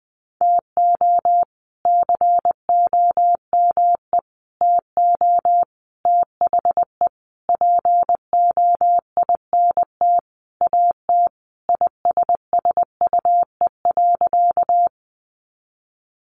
Morse code, heard as a keyed tone, about 20 wpm, 700 Hz. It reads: TO COME TO THE POINT AT ISSUE.
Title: Sentences from Top 900 Words 20wpm